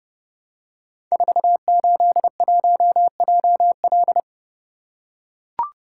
Morse code, heard as a keyed tone, about 29 wpm, 700 Hz. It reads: 481JL
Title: Code Groups - 5 Letter:Numbers - 30wpm